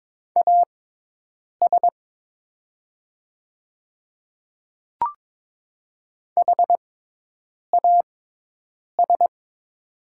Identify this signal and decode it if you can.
Morse code, heard as a keyed tone, about 22 wpm, spaced as for 8 wpm, 700 Hz. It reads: AS HAS
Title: 6x Character Spacing - Top 200 Words - 22wpm